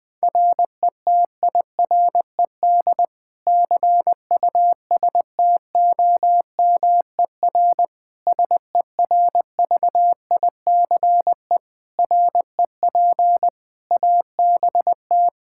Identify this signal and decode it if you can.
Morse code, heard as a keyed tone, about 21 wpm, 700 Hz. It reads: RETIRED CUSTOMER SERVICE REP ABT
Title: Practice QSO - Exchange 3 - More Info - 20wpm